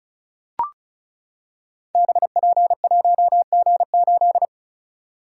Morse code, heard as a keyed tone, about 34 wpm, 700 Hz. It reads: BP1G8
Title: Code Groups - 5 Letter:Numbers - 35wpm